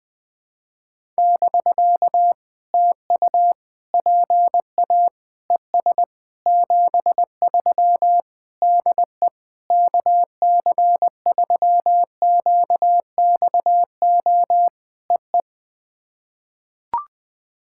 Morse code, heard as a keyed tone, about 21 wpm, 700 Hz. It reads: <BK> TU PA ES 73 DE KC3QXO EE
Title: POTA QSO Practice - 20wpm